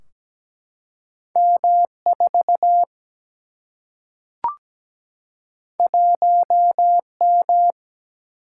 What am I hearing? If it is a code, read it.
Morse code, heard as a keyed tone, about 17 wpm, 700 Hz. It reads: M4 1M
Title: Single-Letter Single-Number 17wpm